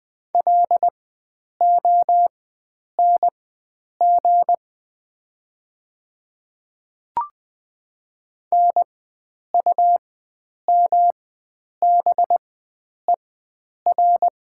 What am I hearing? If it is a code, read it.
Morse code, heard as a keyed tone, about 20 wpm, spaced as for 9 wpm, 700 Hz. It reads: LONG NUMBER
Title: 4x Character Spacing - Top 100 Words - 20wpm